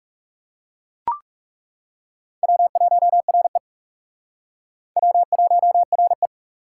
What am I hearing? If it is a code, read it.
Morse code, heard as a keyed tone, about 37 wpm, 700 Hz. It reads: W1RE W1RE
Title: Speed Racing - USA Call Signs 30wpm